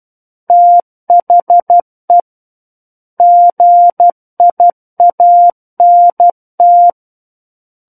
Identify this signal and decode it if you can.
Morse code, heard as a keyed tone, about 12 wpm, 700 Hz. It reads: THE GIANT
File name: Book One / 03-Chapter 3 - 12 WPM